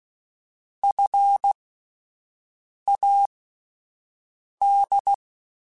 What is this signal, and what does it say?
Morse code, heard as a keyed tone, about 16 wpm, spaced as for 5 wpm, 800 Hz. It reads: FAD
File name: Alphabetical / Alphabetical mp3s / Fs @5